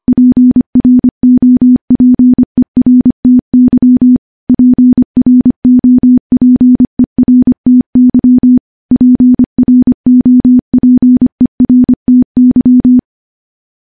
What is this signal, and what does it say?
Morse code, heard as a keyed tone, about 25 wpm, 250 Hz. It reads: PROPERTY PROPERTY PROPERTY
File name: morse-training-25WPM